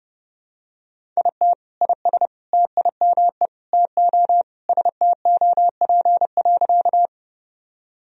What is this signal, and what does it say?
Morse code, heard as a keyed tone, about 32 wpm, 700 Hz. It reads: IT IS TIME TO STOP.
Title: Sentences from Top 1000 Words - Review Part 1 - 30wpm